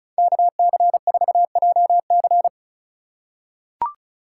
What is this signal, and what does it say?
Morse code, heard as a keyed tone, about 36 wpm, 700 Hz. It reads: KC4JC
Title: USA Call Signs - 4x Character Spacing - 35wpm